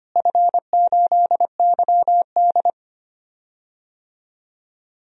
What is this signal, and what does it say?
Morse code, heard as a keyed tone, about 25 wpm, 700 Hz. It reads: F8YD
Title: DXCC-100 Call Signs - 4 Characters - Repeated 2x - 25wpm